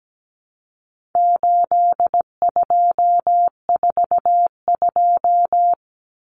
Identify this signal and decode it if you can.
Morse code, heard as a keyed tone, about 17 wpm, 700 Hz. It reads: 8242
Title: Common Cut Numbers - 4-Digit 17wpm